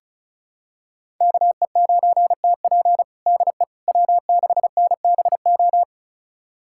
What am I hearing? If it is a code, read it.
Morse code, heard as a keyed tone, about 36 wpm, 700 Hz. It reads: KE9TP DE W6NBO
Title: USA Callsign DE Callsign - 5 Characters - 35wpm